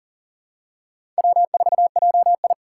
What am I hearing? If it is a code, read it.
Morse code, heard as a keyed tone, about 40 wpm, 700 Hz. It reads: W4JI
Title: USA Call Signs - 2x Character Spacing - 40wpm